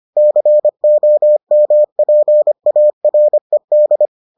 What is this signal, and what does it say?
Morse code, heard as a keyed tone, about 25 wpm, 600 Hz. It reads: COMPARED